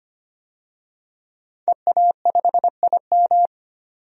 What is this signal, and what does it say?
Morse code, heard as a keyed tone, about 25 wpm, 700 Hz. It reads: EA5IM
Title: DXCC-100 Call Signs - 5 Characters - 25wpm